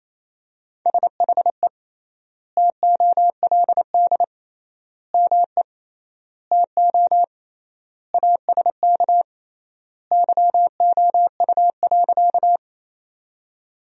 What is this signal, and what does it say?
Morse code, heard as a keyed tone, about 28 wpm, 700 Hz. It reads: SHE TOLD ME TO ASK YOU.
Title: 3x Word-Spacing - Sentences from Top 400 Words - 28wpm